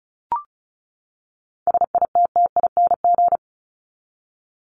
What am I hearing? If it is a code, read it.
Morse code, heard as a keyed tone, about 34 wpm, 700 Hz. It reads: SITTING
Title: ING Suffix - Top 100 Words with Base Words 35wpm